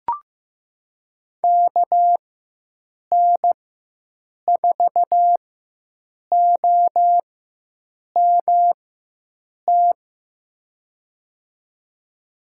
Morse code, heard as a keyed tone, about 15 wpm, spaced as for 7 wpm, 700 Hz. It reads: KN4OMT
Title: USA Call Signs - 6 Characters - 4x Character Spacing - 15wpm